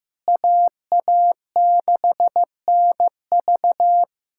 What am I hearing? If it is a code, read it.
Morse code, heard as a keyed tone, about 15 wpm, 700 Hz. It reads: AA6NV
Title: USA Callsign DE Callsign - 5 Characters - 15wpm